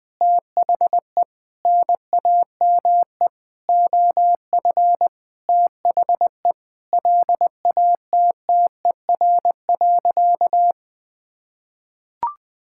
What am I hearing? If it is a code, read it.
Morse code, heard as a keyed tone, about 21 wpm, 700 Hz. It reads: THE NAME OF THE LATTER.
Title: Sentences from Top 1600 Words 20wpm